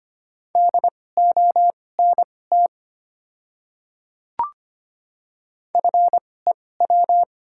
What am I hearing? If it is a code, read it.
Morse code, heard as a keyed tone, about 25 wpm, spaced as for 18 wpm, 700 Hz. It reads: DONT FEW
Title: 2x Character Spacing - Top 500 Words - 25wpm